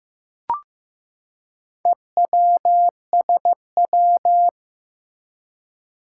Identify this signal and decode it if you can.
Morse code, heard as a keyed tone, about 15 wpm, 700 Hz. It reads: EWSW
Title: Four Challenging Letter-Number Combinations 15wpm